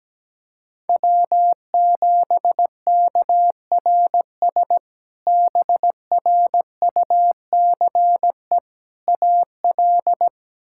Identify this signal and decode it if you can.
Morse code, heard as a keyed tone, about 17 wpm, 700 Hz. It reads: W7KRS BRUCE AL
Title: K1USN SST QSO Practice - 17wpm